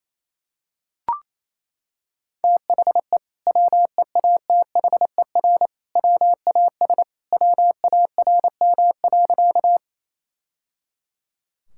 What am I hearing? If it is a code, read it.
Morse code, heard as a keyed tone, about 27 wpm, 700 Hz. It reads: THE WEATHER WAS WARM.
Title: Sentences from Top 1000 Words - Review Part 5 - 28wpm